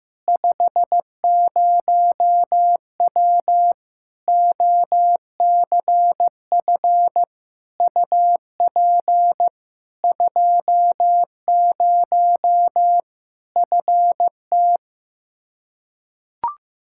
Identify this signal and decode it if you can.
Morse code, heard as a keyed tone, about 16 wpm, 700 Hz. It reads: 50W OCF UP 20 FT
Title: Practice QSO - Exchange 2 - Info - 15wpm